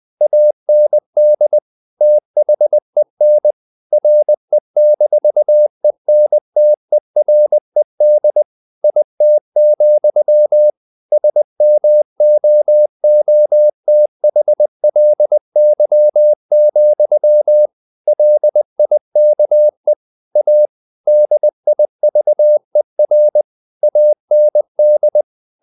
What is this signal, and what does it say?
Morse code, heard as a keyed tone, about 20 wpm, 600 Hz. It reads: AND THEN RE-ENTERED IT, SMOOTHLY, LIKE A DIVER AND
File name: the old man and the sea / the old man and the sea part 1 20wpm